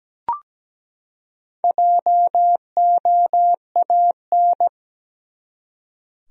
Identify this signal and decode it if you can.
Morse code, heard as a keyed tone, about 17 wpm, 700 Hz. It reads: JOAN
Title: USA Common Names - Max 5 Letters - 17wpm